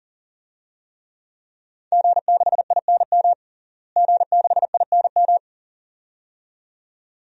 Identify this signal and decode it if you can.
Morse code, heard as a keyed tone, about 37 wpm, 690 Hz. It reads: G6INM G6INM
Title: DXCC-100 Call Signs - 5 Characters - Repeated 2x - 40wpm